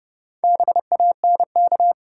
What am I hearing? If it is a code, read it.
Morse code, heard as a keyed tone, about 29 wpm, 700 Hz. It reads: BANK
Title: Sets of 5 Words - New Word Drill - 30wpm